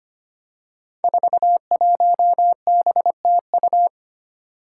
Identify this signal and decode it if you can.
Morse code, heard as a keyed tone, about 25 wpm, 700 Hz. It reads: 41BTU
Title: Five Challenging Letter-Number Combinations 25wpm